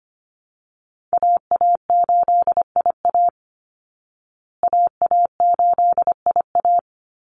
Morse code, heard as a keyed tone, about 25 wpm, 700 Hz. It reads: AA8IA AA8IA
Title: Call Signs Repeated 2x - 25wpm